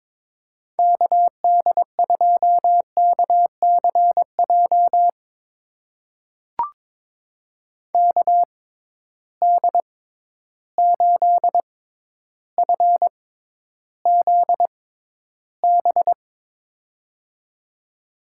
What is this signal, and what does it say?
Morse code, heard as a keyed tone, about 22 wpm, spaced as for 11 wpm, 700 Hz. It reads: KD2KCJ KD8FZB
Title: USA Call Signs - 6 Characters - 6x Character Spacing - 22wpm